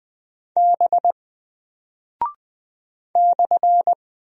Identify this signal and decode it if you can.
Morse code, heard as a keyed tone, about 20 wpm, 700 Hz. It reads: B /
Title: Single Character (Letter:Number:Punctuation:Prosign) Alt 20wpm